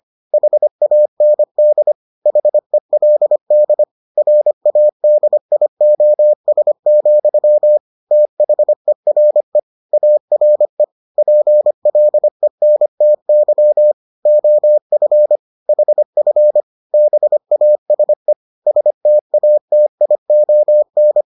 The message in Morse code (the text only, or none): HAND HELD RADIOS, THERE ARE PLENTY OF HF BASE STATION